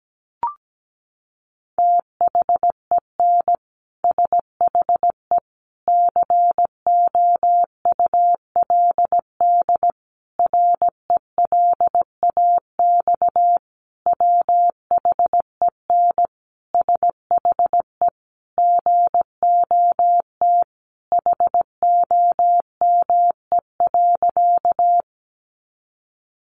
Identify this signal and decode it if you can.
Morse code, heard as a keyed tone, about 17 wpm, 700 Hz. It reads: THEN SHE COULD RELAX WHEN SHE GOT HOME.
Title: Sentences from Top 1400 Words 17wpm